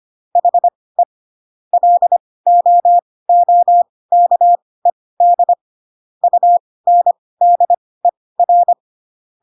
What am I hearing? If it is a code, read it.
Morse code, heard as a keyed tone, about 25 wpm, spaced as for 18 wpm, 700 Hz. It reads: HE LOOKED UNDER
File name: Pinocchio / Pinocchio ch1 12 25 ik1hge